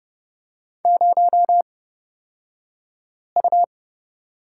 Morse code, heard as a keyed tone, about 30 wpm, spaced as for 5 wpm, 700 Hz. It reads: 0U
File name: Lesson 37d - Letter x - Two Characters 30wpm